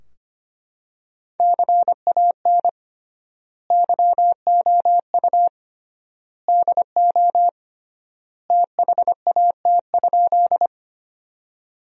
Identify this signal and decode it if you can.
Morse code, heard as a keyed tone, about 25 wpm, 700 Hz. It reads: CAN YOU DO THAT?
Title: 3x Word-Spacing - Sentences from Top 100 Words - 25wpm